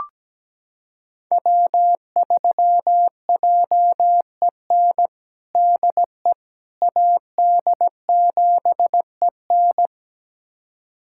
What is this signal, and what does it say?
Morse code, heard as a keyed tone, about 17 wpm, 700 Hz. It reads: W3JEN DE AD7EN
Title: USA Callsign DE Callsign - 5 Characters - 17wpm